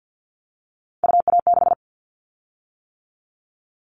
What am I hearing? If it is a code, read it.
Morse code, heard as a keyed tone, about 50 wpm, 700 Hz. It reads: UA6
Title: Callsign Prefixes for Top 50 DXCC - 50wpm